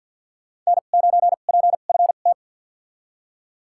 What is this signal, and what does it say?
Morse code, heard as a keyed tone, about 52 wpm, spaced as for 36 wpm, 700 Hz. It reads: N9PFT
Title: USA Call Signs - 5 Characters - 2x Character Spacing - 50wpm